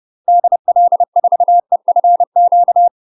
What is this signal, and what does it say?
Morse code, heard as a keyed tone, about 30 wpm, 700 Hz. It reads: DL4EFQ
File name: UK MASTER QSOs TRAINING / 30WPM / UK MASTER QSOs 0019